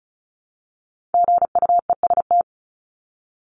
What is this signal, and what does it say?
Morse code, heard as a keyed tone, about 34 wpm, 700 Hz. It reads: GUEST
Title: Top 1501-1750 Words 35wpm